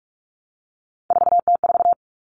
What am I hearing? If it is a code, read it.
Morse code, heard as a keyed tone, about 44 wpm, 700 Hz. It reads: 4T4
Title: Common Cut Numbers - 3-Digit 45wpm